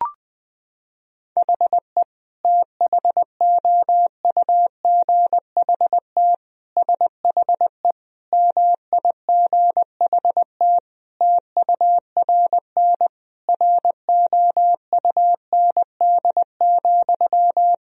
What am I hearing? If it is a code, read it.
Morse code, heard as a keyed tone, about 20 wpm, 700 Hz. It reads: HE THOUGHT SHE MIGHT TURN ROUND,
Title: Sentences from Top 1000 Words - Review Part 1 - 20wpm